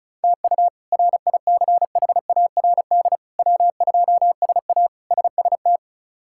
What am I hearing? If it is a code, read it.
Morse code, heard as a keyed tone, about 34 wpm, 700 Hz. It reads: TU RICHARD W2SA SST
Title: K1USN SST QSO Practice - 35wpm